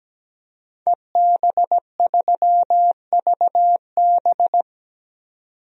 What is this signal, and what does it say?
Morse code, heard as a keyed tone, about 17 wpm, 700 Hz. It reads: EB3VB
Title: DXCC-100 Call Signs - 5 Characters - 17wpm